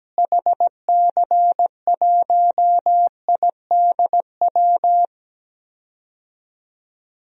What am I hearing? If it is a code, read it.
Morse code, heard as a keyed tone, about 17 wpm, 700 Hz. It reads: HC1IDW
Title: DXCC-100 Call Signs - 6 Characters - Repeated 3x - 17wpm